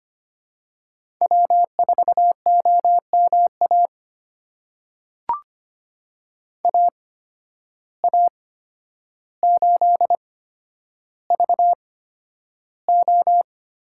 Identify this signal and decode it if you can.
Morse code, heard as a keyed tone, about 25 wpm, spaced as for 11 wpm, 700 Hz. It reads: W4OMA AA8VO
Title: USA Call Signs - 5 Characters - 8x Character Spacing - 25wpm